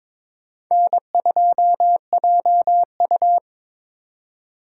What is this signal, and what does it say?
Morse code, heard as a keyed tone, about 25 wpm, 700 Hz. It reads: N2JU
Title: Speed Racing - USA 1x2 Call Signs 22wpm